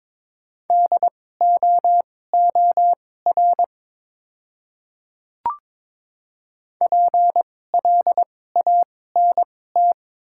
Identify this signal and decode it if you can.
Morse code, heard as a keyed tone, about 22 wpm, spaced as for 16 wpm, 700 Hz. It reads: DOOR PLANT
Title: 2x Character Spacing - Top 300 Words - 22wpm